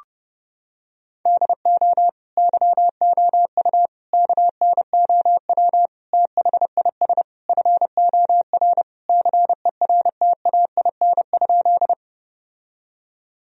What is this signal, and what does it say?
Morse code, heard as a keyed tone, about 30 wpm, 700 Hz. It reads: DO YOU KNOW THIS FOR CERTAIN?
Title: Sentences from Top 1000 Words - Review Part 2 - 30wpm